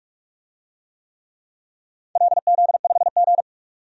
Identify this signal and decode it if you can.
Morse code, heard as a keyed tone, about 46 wpm, 700 Hz. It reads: LZ5G
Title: DXCC-100 Call Signs - 4 Characters - 45wpm